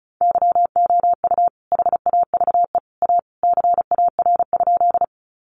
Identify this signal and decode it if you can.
Morse code, heard as a keyed tone, about 34 wpm, 700 Hz. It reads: YOU HAVE A CAR?
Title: Sentences from Top 300 Words 35wpm